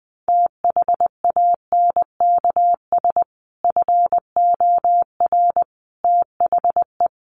Morse code, heard as a keyed tone, about 20 wpm, 700 Hz. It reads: THANKS FOR THE
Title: Sentences from Top 1500 Words 20wpm